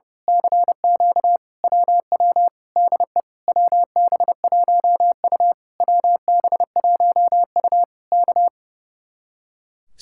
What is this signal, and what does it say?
Morse code, heard as a keyed tone, about 30 wpm, 700 Hz. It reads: CQ WW DE WB1U WB1U K
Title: Speed-Racing Common QSO Phrases - 30wpm